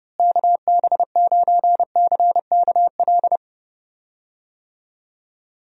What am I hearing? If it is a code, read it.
Morse code, heard as a keyed tone, about 29 wpm, 700 Hz. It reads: KB9CKL
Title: USA Call Signs - 6 Characters - Repeated 2x - 30wpm